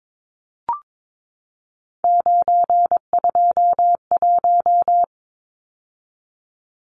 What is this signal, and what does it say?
Morse code, heard as a keyed tone, about 22 wpm, 700 Hz. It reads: 921
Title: Numbers - 3-Digit 22wpm